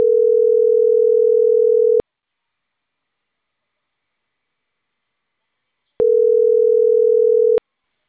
Ringback tone (North America), 440 + 480 Hz.